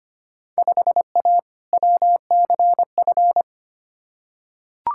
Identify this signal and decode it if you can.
Morse code, heard as a keyed tone, about 25 wpm, 700 Hz. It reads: 5A WCF
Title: Field Day Exchanges Encore - 25wpm